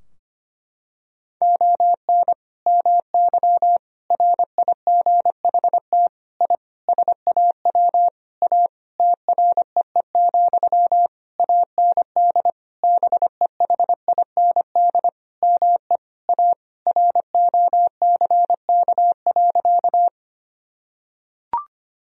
Morse code, heard as a keyed tone, about 25 wpm, 700 Hz. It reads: ON MY RIGHT I SAW A TREE, AND BEHIND ME A ROCK.
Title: Sentences from Top 1000 Words - Review Part 2 - 25wpm